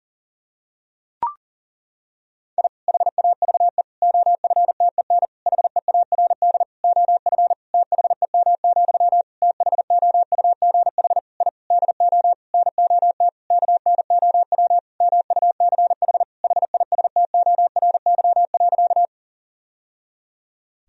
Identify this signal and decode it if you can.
Morse code, heard as a keyed tone, about 37 wpm, 700 Hz. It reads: I HAVE OFTEN HEARD OF THEM, THOUGH I DO NOT KNOW MUCH HISTORY.
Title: Sentences from Top 1000 Words - Review Part 2 - 40wpm